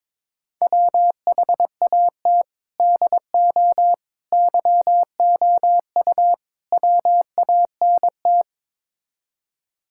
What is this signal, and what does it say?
Morse code, heard as a keyed tone, about 21 wpm, 700 Hz. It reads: WHAT DO YOU WANT
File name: practice / 22wpm / Build A Phrase - 5-Word Phrases - 22wpm